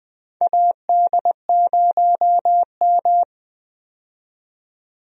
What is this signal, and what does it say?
Morse code, heard as a keyed tone, about 20 wpm, 700 Hz. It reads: AD0M
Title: USA Call Signs - 4 Characters - 20wpm